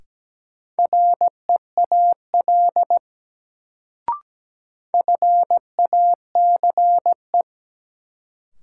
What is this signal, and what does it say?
Morse code, heard as a keyed tone, about 17 wpm, 700 Hz. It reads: REAL FACE
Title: Sentences from Top 300 Words - New Word Drill - 17wpm